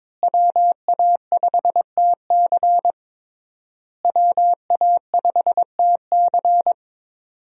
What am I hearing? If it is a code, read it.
Morse code, heard as a keyed tone, about 21 wpm, 700 Hz. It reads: WA5TC WA5TC
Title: USA Call Signs - 5 Characters - Repeated 3x - 22wpm